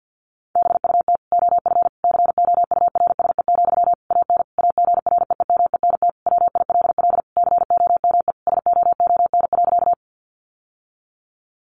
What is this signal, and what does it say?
Morse code, heard as a keyed tone, about 50 wpm, 700 Hz. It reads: BUT OF COURSE, AN AGREEMENT WILL COME SOON.
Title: Sentences from Top 1500 Words 50wpm